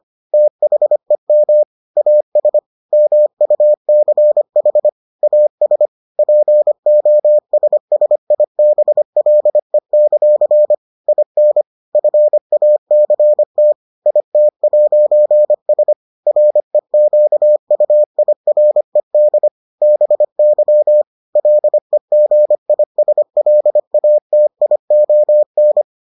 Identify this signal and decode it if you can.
Morse code, heard as a keyed tone, about 25 wpm, 600 Hz. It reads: THEM AS MUCH AS POSSIBLE; IN FACT IT'S REQUIRED BY LEGISLATION